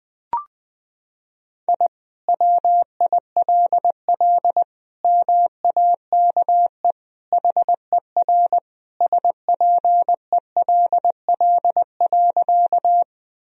Morse code, heard as a keyed tone, about 20 wpm, 700 Hz. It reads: I WILL MAKE HER SPELL.
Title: Sentences from Top 1000 Words - Review Part 1 - 20wpm